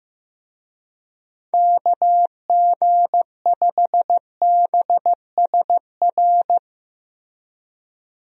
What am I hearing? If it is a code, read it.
Morse code, heard as a keyed tone, about 16 wpm, 700 Hz. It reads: KG5BSR
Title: USA Call Signs - 6 Characters - Repeated 3x - 15wpm